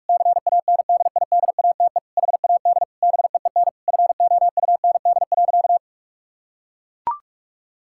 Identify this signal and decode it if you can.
Morse code, heard as a keyed tone, about 41 wpm, 700 Hz. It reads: CANDIDATE HAD BEEN FOUND.
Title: Sentences from Top 1600 Words 45wpm